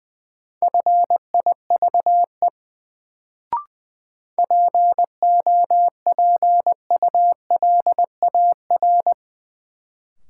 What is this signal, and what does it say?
Morse code, heard as a keyed tone, about 20 wpm, 700 Hz. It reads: FIVE POPULAR
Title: Sets of 5 Words - New Word Drill - 20wpm